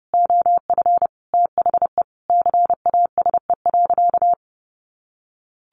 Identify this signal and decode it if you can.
Morse code, heard as a keyed tone, about 30 wpm, 700 Hz. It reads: OF THE CASE.
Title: Sentences from Top 1400 Words 30wpm